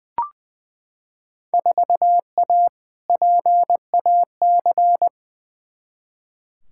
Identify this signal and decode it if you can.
Morse code, heard as a keyed tone, about 20 wpm, 700 Hz. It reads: 4A PAC
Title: Field Day Exchanges Encore - 20wpm